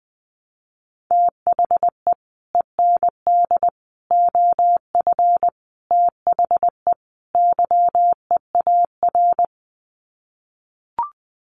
Morse code, heard as a keyed tone, about 20 wpm, 700 Hz. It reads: THE END OF THE YEAR
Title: Sets of 5 Words 20wpm